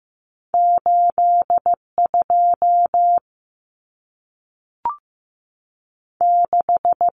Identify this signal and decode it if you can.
Morse code, heard as a keyed tone, about 15 wpm, 700 Hz. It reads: 8 2 6
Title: Numbers - 2-Digit 15wpm